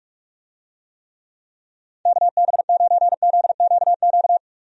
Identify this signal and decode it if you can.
Morse code, heard as a keyed tone, about 41 wpm, 700 Hz. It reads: KB9ZQQ
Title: USA Call Signs - 6 Characters - Repeated 2x - 45wpm